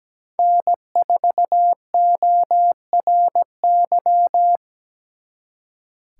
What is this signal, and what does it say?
Morse code, heard as a keyed tone, about 20 wpm, 700 Hz. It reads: N4ORY
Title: Speed Racing - USA 1x3 Call Signs 17wpm